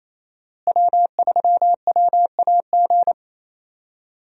Morse code, heard as a keyed tone, about 28 wpm, 700 Hz. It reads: W3WAG
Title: USA Call Signs - 5 Characters - Repeated 2x - 28wpm